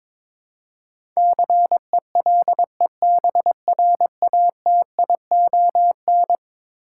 Morse code, heard as a keyed tone, about 22 wpm, 700 Hz. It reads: CELEBRATION
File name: practice / 22wpm / Eleven Letter Words - 22wpm